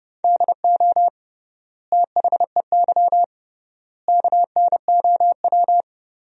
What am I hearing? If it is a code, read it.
Morse code, heard as a keyed tone, about 30 wpm, 700 Hz. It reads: DO THEY KNOW
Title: 3x Word-Spacing - Sentences from Top 100 Words - 30wpm